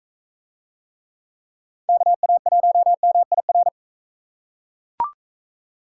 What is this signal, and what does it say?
Morse code, heard as a keyed tone, about 38 wpm, 700 Hz. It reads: KA1MIR